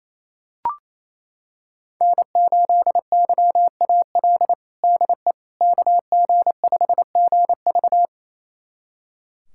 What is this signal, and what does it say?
Morse code, heard as a keyed tone, about 28 wpm, 700 Hz. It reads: N8YAL DE KG5GV
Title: USA Callsign DE Callsign - 5 Characters - 28wpm